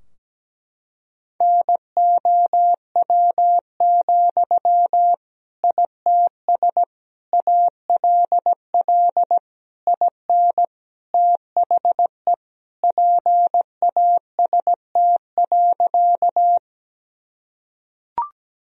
Morse code, heard as a keyed tone, about 17 wpm, 700 Hz. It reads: NOW, ITS ALL IN THE PAST.
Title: Sentences from Top 1000 Words - Review Part 3 - 17wpm